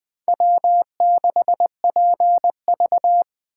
Morse code, heard as a keyed tone, about 20 wpm, 700 Hz. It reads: W6PV
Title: Speed Racing - USA 1x2 Call Signs 17wpm